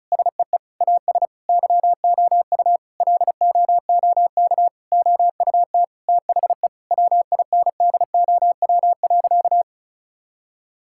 Morse code, heard as a keyed tone, about 34 wpm, 700 Hz. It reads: SEE AS YOU LOOK OUT THE WINDOW.
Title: Sentences from Top 1600 Words 35wpm